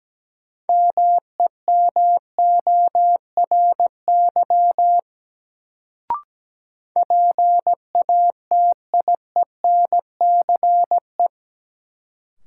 Morse code, heard as a keyed tone, about 17 wpm, 700 Hz. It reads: MEMORY PATIENCE
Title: Sentences from Top 1300 Words - New Word Drill Alt - 17wpm